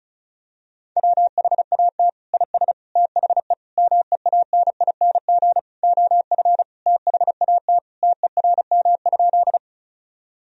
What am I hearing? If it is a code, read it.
Morse code, heard as a keyed tone, about 34 wpm, 700 Hz. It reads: WHAT IS THE MEANING OF THAT TERM?
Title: Sentences from Top 1000 Words - Review Part 4 - 35wpm